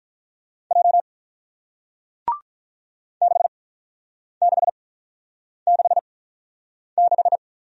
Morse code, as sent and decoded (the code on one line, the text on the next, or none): .--- / -.... -.... -.... -....
J 6666